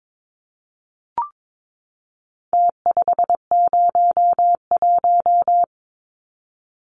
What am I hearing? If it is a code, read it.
Morse code, heard as a keyed tone, about 22 wpm, 700 Hz. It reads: T501
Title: Numbers - Contest Style - 4-Digit 22wpm